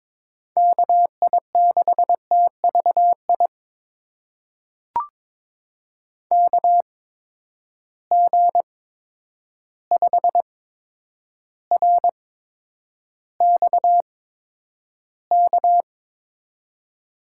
Morse code, heard as a keyed tone, about 22 wpm, spaced as for 9 wpm, 700 Hz. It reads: KI6TVI KG5RXK
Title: USA Call Signs - 6 Characters - 8x Character Spacing - 22wpm